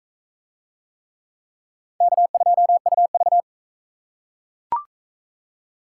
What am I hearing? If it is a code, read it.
Morse code, heard as a keyed tone, about 41 wpm, 700 Hz. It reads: K2UV